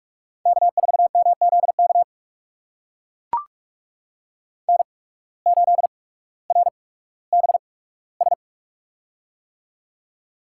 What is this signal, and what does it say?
Morse code, heard as a keyed tone, about 42 wpm, spaced as for 19 wpm, 700 Hz. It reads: K4MZK N8RBS
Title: USA Call Signs - 5 Characters - 8x Character Spacing - 45wpm